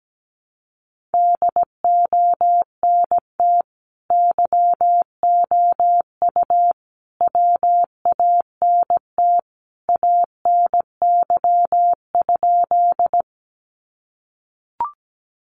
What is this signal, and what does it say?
Morse code, heard as a keyed tone, about 17 wpm, 700 Hz. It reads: DONT YOU WANT ANY?
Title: Sentences from Top 300 Words 17wpm